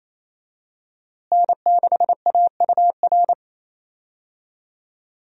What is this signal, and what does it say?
Morse code, heard as a keyed tone, about 28 wpm, 700 Hz. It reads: N6AUR
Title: USA Call Signs - 5 Characters - Repeated 3x - 28wpm